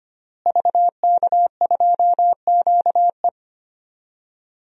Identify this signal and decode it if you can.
Morse code, heard as a keyed tone, about 25 wpm, 700 Hz. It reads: VK2QE
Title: DXCC-100 Call Signs - 5 Characters - Repeated 3x - 25wpm